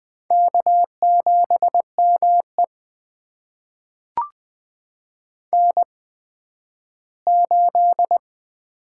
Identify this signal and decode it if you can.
Morse code, heard as a keyed tone, about 20 wpm, spaced as for 12 wpm, 700 Hz. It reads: K7ME N8
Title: USA Call Signs - 8x Character Spacing - 20wpm